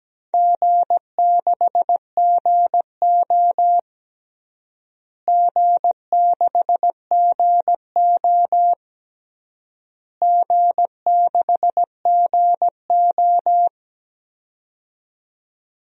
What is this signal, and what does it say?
Morse code, heard as a keyed tone, about 17 wpm, 700 Hz. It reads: G6GO G6GO G6GO
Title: DXCC-100 Call Signs - 4 Characters - Repeated 3x - 17wpm